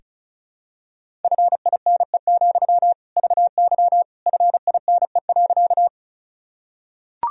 Morse code, heard as a keyed tone, about 36 wpm, 700 Hz. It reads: FINE, VY FINE.
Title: Sentences from Top 500 Words and Abbreviations 35wpm